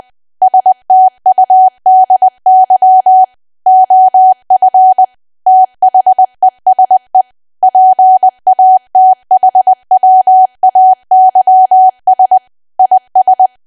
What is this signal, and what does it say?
Morse code, heard as a keyed tone, about 20 wpm, 750 Hz. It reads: STUDY OF THESE PATHWAYS IS
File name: cwa / W1AW-Level3 / 20WPM